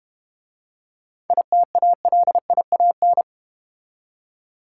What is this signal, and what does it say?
Morse code, heard as a keyed tone, about 30 wpm, 700 Hz. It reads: ITALIAN
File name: Easy Words - Speed-Racing with Hints - 25wpm